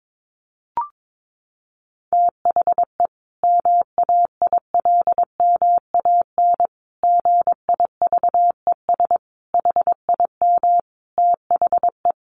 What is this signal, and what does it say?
Morse code, heard as a keyed tone, about 22 wpm, 700 Hz. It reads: THE MAILMAN GIVES HIM THE
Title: Easy Phrases 22wpm